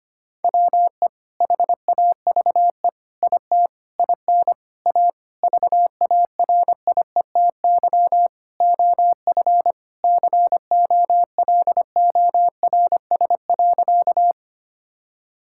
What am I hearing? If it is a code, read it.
Morse code, heard as a keyed tone, about 25 wpm, 700 Hz. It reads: WE HAVE IT IN A VARIETY OF COLORS.
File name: practice / 25wpm / Sentences with 4-Word Phrases - 25wpm